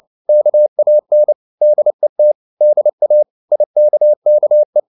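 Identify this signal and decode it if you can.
Morse code, heard as a keyed tone, about 29 wpm, 600 Hz. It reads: KAN DET DA IKKE
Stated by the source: Fast (QRQ) Morse at 29 wpm, from CW med Gnister.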